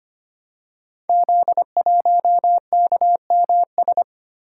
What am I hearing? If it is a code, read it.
Morse code, heard as a keyed tone, about 25 wpm, 700 Hz. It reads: Z1KMS
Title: Code Groups - 5 Letter:Numbers - 25wpm